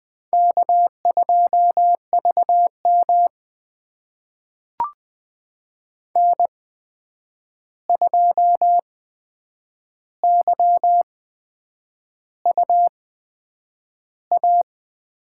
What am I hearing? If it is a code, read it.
Morse code, heard as a keyed tone, about 20 wpm, spaced as for 8 wpm, 700 Hz. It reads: K2VM N2YUA
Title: USA Call Signs - 8x Character Spacing - 20wpm